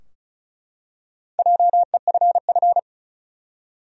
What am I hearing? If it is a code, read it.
Morse code, heard as a keyed tone, about 36 wpm, 700 Hz. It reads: JEFF
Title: USA Common Names - Max 4 Letters - 35wpm